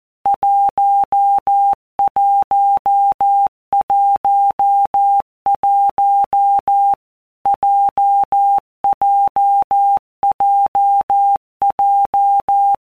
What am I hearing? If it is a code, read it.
Morse code, heard as a keyed tone, about 14 wpm, 800 Hz. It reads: 1111 JJJJ
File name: Basic Letter Groups By WPM's / Groups@13 / JZ17@13